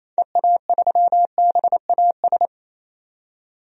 Morse code, heard as a keyed tone, about 28 wpm, 700 Hz. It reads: EA3BAS
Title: DXCC-100 Call Signs - 6 Characters - 28wpm